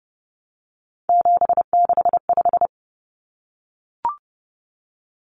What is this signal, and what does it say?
Morse code, heard as a keyed tone, about 30 wpm, 700 Hz. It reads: 765
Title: Numbers - 3-Digit 30wpm